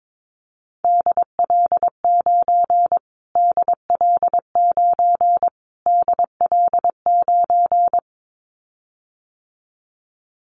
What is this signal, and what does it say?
Morse code, heard as a keyed tone, about 22 wpm, 700 Hz. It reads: DL9 DL9 DL9
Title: Callsign Prefixes for Top 25 DXCC - Repeated 3x - 22wpm